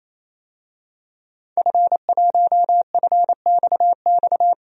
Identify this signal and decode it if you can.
Morse code, heard as a keyed tone, about 28 wpm, 700 Hz. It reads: F1FXX
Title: DXCC-100 Call Signs - 5 Characters - 28wpm